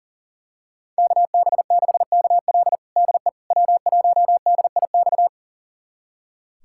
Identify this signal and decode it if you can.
Morse code, heard as a keyed tone, about 40 wpm, 700 Hz. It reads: KB6KL DE W1DIX
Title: USA Callsign DE Callsign 40wpm